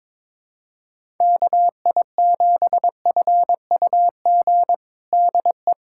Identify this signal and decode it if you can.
Morse code, heard as a keyed tone, about 22 wpm, 700 Hz. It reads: KI7FUG DE
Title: USA Callsign DE Callsign - 6 Characters - 22wpm